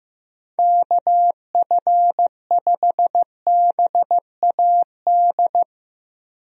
Morse code, heard as a keyed tone, about 15 wpm, 700 Hz. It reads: KF5BAD
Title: USA Call Signs - 6 Characters - Repeated 2x - 15wpm